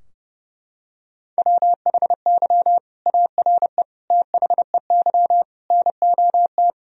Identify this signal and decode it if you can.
Morse code, heard as a keyed tone, about 32 wpm, 700 Hz. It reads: WHY ARE THEY NOT
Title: Sentences from Top 1000 Words - Review Part 4 - 30wpm